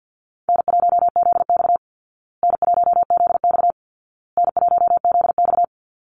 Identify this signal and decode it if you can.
Morse code, heard as a keyed tone, about 50 wpm, 700 Hz. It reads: N1ZX N1ZX N1ZX
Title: Easy Call Signs Repeated 3x - 50wpm